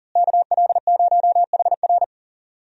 Morse code, heard as a keyed tone, about 40 wpm, 700 Hz. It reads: KL0HR
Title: USA Call Signs - 5 Characters - Repeated 3x - 40wpm